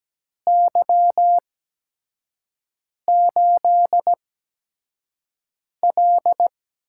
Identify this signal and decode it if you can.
Morse code, heard as a keyed tone, about 17 wpm, spaced as for 5 wpm, 700 Hz. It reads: Y8L
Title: USA Call Signs - 4 Characters - 8x Character Spacing - 17wpm